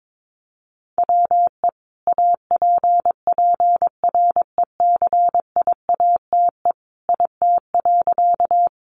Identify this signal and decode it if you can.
Morse code, heard as a keyed tone, about 22 wpm, 700 Hz. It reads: WE APPRECIATE IT.
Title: Sentences from Top 1300 Words 22wpm